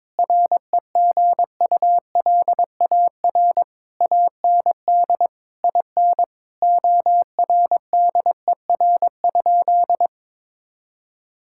Morse code, heard as a keyed tone, about 22 wpm, 700 Hz. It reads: REGULAR AND IN ORDER?
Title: Sentences from Top 1000 Words - Review Part 6 - 22wpm